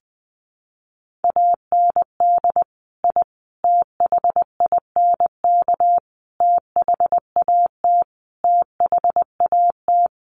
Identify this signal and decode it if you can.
Morse code, heard as a keyed tone, about 20 wpm, 700 Hz. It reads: AND I THINK THAT THAT
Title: Sets of 5 Words Encore 20wpm